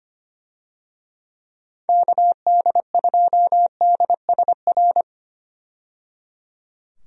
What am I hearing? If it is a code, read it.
Morse code, heard as a keyed tone, about 25 wpm, 700 Hz. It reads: KD2DSR
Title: USA Call Signs - 6 Characters - 25wpm